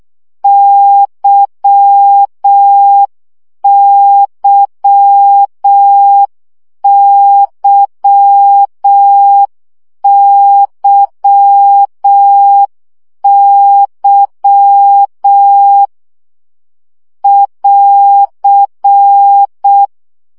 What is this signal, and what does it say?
Morse code, heard as a keyed tone, about 6 wpm, 800 Hz. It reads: YYYYY <AR>
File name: CW gyakorlas y 5 baud